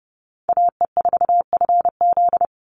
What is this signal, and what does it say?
Morse code, heard as a keyed tone, about 30 wpm, 700 Hz. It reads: AE4FZ
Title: NA SOTA Chaser Callsigns 30wpm